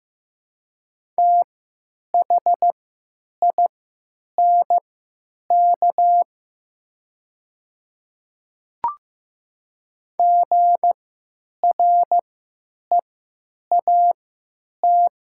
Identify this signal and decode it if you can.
Morse code, heard as a keyed tone, about 16 wpm, spaced as for 9 wpm, 700 Hz. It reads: THINK GREAT
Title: 3x Character Spacing - Top 200 Words - 15wpm